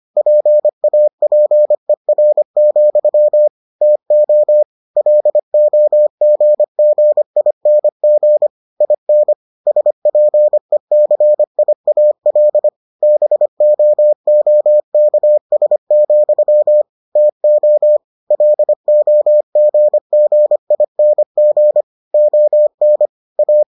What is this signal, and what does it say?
Morse code, heard as a keyed tone, about 25 wpm, 600 Hz. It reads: PAPER, TO LOGGING IN SPECIAL BOOKS, TO LOGGING ON A